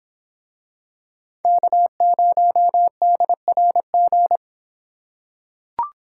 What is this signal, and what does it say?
Morse code, heard as a keyed tone, about 23 wpm, 700 Hz. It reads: K0DRG